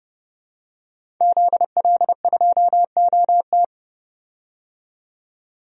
Morse code, heard as a keyed tone, about 31 wpm, 700 Hz. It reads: ZL2OT
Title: DXCC-100 Call Signs - 5 Characters - 30wpm